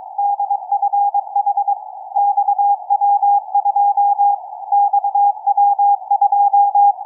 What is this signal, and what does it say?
Morse code, heard as a keyed tone, about 22 wpm, 810 Hz. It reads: DFH XW2 XW2
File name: XW2E-CAS3E-VERT-01 18h02 utc 145mhz910 cw